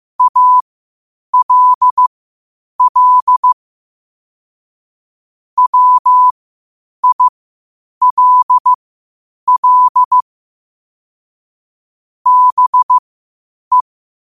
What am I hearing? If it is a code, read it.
Morse code, heard as a keyed tone, about 15 wpm, spaced as for 8 wpm, 1000 Hz. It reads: ALL WILL BE
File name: qotdfiles / 8wpm